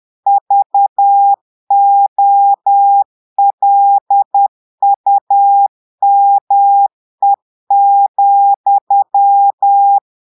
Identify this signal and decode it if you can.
Morse code, heard as a keyed tone, about 10 wpm, 800 Hz. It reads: VOLUME,
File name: Portugais - texte en clair 1 - vitesse - 10 mots minute